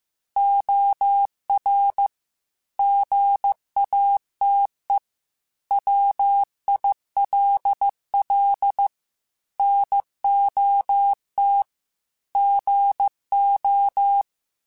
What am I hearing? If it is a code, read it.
Morse code, heard as a keyed tone, about 15 wpm, 800 Hz. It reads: OR GATE WILL NOT GO
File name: Technical Subjects / Technical Subjects-mp3s / OR Gate @13